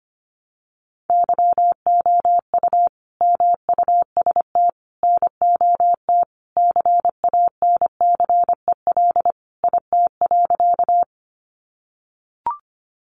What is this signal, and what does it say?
Morse code, heard as a keyed tone, about 25 wpm, 700 Hz. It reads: YOU MUST NOT CANCEL IT.
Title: Sentences from Top 1200 Words 25wpm